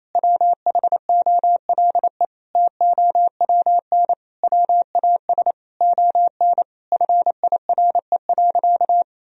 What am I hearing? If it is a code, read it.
Morse code, heard as a keyed tone, about 28 wpm, 700 Hz. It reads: WHOLE TOWN WAS ON FIRE.
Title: Sentences from Top 1000 Words - Review Part 2 - 28wpm